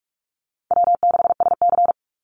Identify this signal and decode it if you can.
Morse code, heard as a keyed tone, about 44 wpm, 700 Hz. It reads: W6SC
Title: Call Signs Repeated 2x - 45wpm